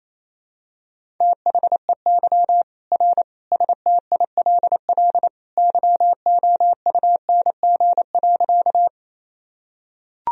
Morse code, heard as a keyed tone, about 28 wpm, 700 Hz. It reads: THEY R STILL YOUNG.
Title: Sentences from Top 400 Words and Abbreviations 28wpm